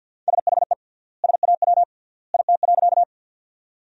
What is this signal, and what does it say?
Morse code, heard as a keyed tone, about 47 wpm, 700 Hz. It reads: SHE SAW IT.
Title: 3x Word-Spacing - Sentences from Top 300 Words - 50wpm